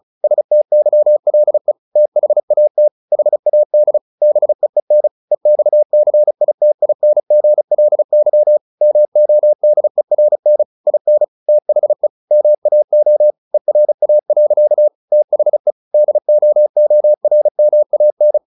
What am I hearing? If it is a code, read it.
Morse code, heard as a keyed tone, about 34 wpm, 600 Hz. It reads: STYLE THAT HAD BEEN EXCITINGLY MODERN IN THE MAO ERA. THE DOORMAN